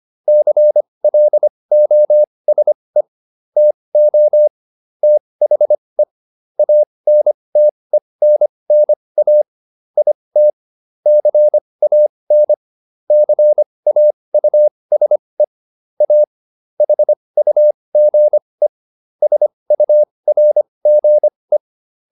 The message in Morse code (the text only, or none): CLOSE TO THE ANTENNA IT CAN CAUSE A HUGE SURGE